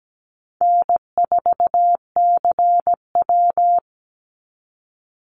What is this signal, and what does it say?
Morse code, heard as a keyed tone, about 17 wpm, 700 Hz. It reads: N4CW
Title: Easy Call Signs Repeated 2x - 17wpm